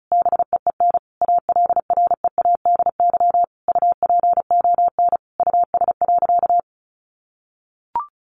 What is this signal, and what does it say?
Morse code, heard as a keyed tone, about 34 wpm, 700 Hz. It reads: BEEN ALREADY UPON US.
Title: Sentences from Top 1500 Words 35wpm